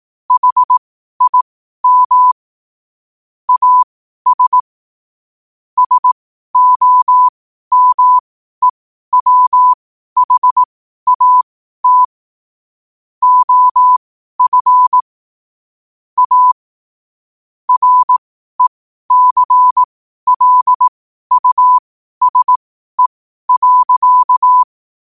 Morse code, heard as a keyed tone, about 18 wpm, spaced as for 12 wpm, 1000 Hz. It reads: HIM AS SOMEWHAT OF A RECLUSE.
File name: qotdfiles / 12wpm